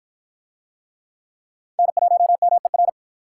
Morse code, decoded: N1MER

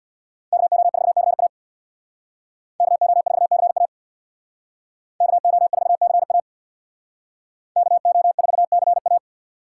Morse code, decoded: KO4CA KO4CA KO4CA KO4CA

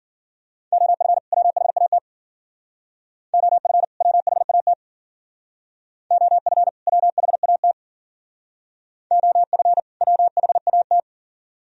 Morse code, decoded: OF WHAT OF WHAT OF WHAT OF WHAT

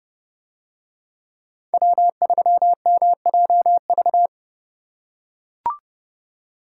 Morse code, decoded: W3MJV

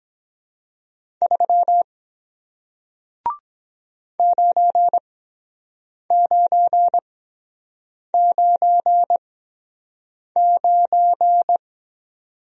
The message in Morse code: 3 9999

